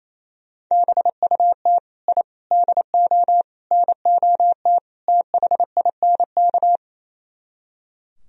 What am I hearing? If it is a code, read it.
Morse code, heard as a keyed tone, about 36 wpm, 700 Hz. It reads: BUT I DO NOT THINK